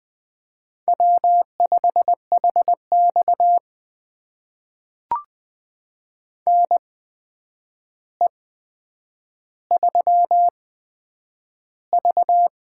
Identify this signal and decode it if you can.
Morse code, heard as a keyed tone, about 20 wpm, spaced as for 9 wpm, 700 Hz. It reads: W5HX NE3V
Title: USA Call Signs - 4 Characters - 8x Character Spacing - 20wpm